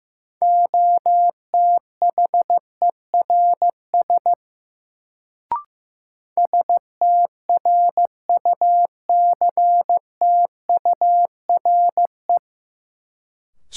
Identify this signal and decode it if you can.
Morse code, heard as a keyed tone, about 15 wpm, 700 Hz. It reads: OTHERS STRUCTURE
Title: Sentences from Top 1500 Words - New Word Drill Alt - 15wpm